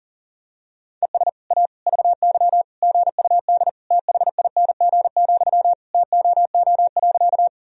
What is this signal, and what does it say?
Morse code, heard as a keyed tone, about 37 wpm, 700 Hz. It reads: ES A VY GUD THING, TOO.
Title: Sentences from Top 200 Words and Abbreviations 40wpm